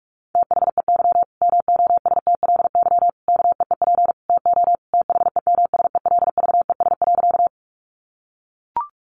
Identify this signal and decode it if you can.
Morse code, heard as a keyed tone, about 44 wpm, 700 Hz. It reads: THEY MOSTLY KEEP TO THEMSELVES.
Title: Sentences from Top 1300 Words 45wpm